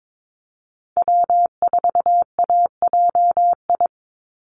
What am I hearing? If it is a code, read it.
Morse code, heard as a keyed tone, about 22 wpm, 700 Hz. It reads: W4AJI
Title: USA Call Signs 22wpm